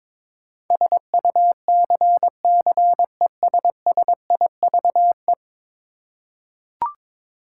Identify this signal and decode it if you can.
Morse code, heard as a keyed tone, about 22 wpm, 700 Hz. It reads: SUCCESSIVE
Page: morse-code-ninja-practice-22wpm